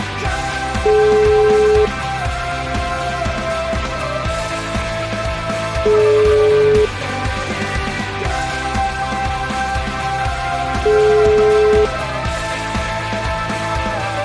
Ringback tone (Europe, 425 Hz), with noise.